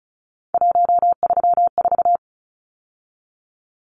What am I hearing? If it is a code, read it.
Morse code, heard as a keyed tone, about 34 wpm, 700 Hz. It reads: 134
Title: Numbers - 3-Digit 35wpm